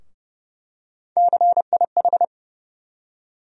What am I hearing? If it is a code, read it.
Morse code, heard as a keyed tone, about 30 wpm, 700 Hz. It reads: CIH
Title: USA Call Sign Postfixes - 3 Letters 30wpm